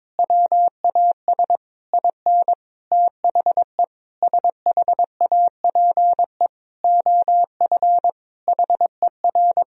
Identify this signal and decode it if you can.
Morse code, heard as a keyed tone, about 22 wpm, 700 Hz. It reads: WAS IN THE SHAPE OF HER
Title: Sentences from Top 1000 Words - Review Part 3 - 22wpm